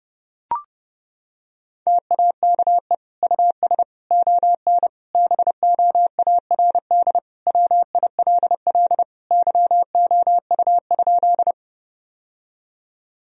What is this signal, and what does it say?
Morse code, heard as a keyed tone, about 32 wpm, 700 Hz. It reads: TAKE US ON BOARD WILL YOU?
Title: Sentences from Top 1000 Words - Review Part 4 - 30wpm